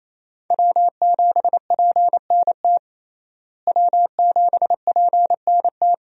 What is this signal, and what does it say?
Morse code, heard as a keyed tone, about 28 wpm, 700 Hz. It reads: W7PNT W7PNT
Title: USA Call Signs - 5 Characters - Repeated 2x - 28wpm